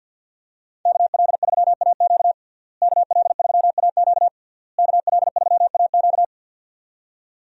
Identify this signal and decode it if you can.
Morse code, heard as a keyed tone, about 50 wpm, 700 Hz. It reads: KL3AQ KL3AQ KL3AQ
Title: USA Call Signs - 5 Characters - Repeated 3x - 50wpm